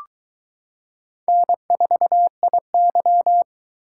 Morse code, heard as a keyed tone, about 22 wpm, 700 Hz. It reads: N4IY